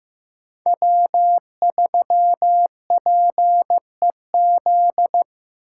Morse code, heard as a keyed tone, about 15 wpm, 700 Hz. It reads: W3PEZ
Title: USA Call Signs - 4x Character Spacing - 15wpm